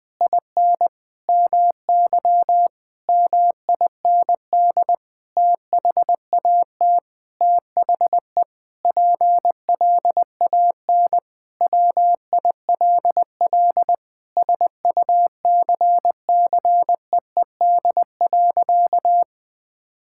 Morse code, heard as a keyed tone, about 20 wpm, 700 Hz. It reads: IN MY MIND THAT THE PLAN WILL SUCCEED.
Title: Sentences with 9-Word Phrases - 20wpm